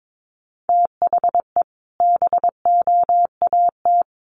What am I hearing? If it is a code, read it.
Morse code, heard as a keyed tone, about 22 wpm, 700 Hz. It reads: THE BOAT
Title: Sentences from Top 1100 Words 22wpm